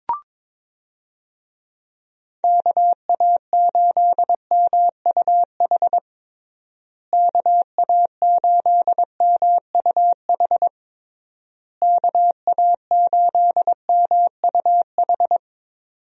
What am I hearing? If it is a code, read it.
Morse code, heard as a keyed tone, about 21 wpm, 700 Hz. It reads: KA8MUH KA8MUH KA8MUH
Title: USA Call Signs - 6 Characters - Repeated 3x - 22wpm